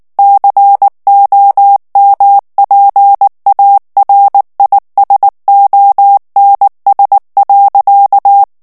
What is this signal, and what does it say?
Morse code, heard as a keyed tone, about 19 wpm, 800 Hz. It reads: COMPARISONS.
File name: ANGLAIS texte clair 2 vitesse 18 mots minute